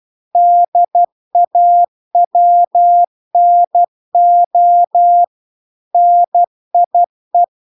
Morse code, heard as a keyed tone, about 12 wpm, 700 Hz. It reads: DAWNO NIE
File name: recordings / 12wpm0000